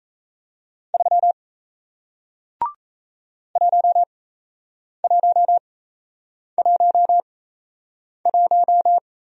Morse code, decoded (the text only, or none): 3 1111